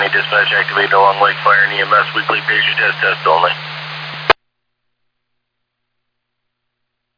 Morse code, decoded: none

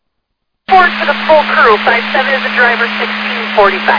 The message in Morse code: none

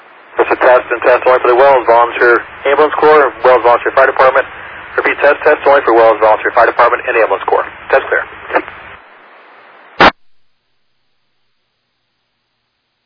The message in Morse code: none

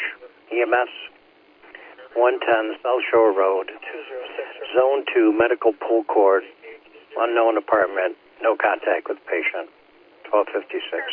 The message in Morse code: none

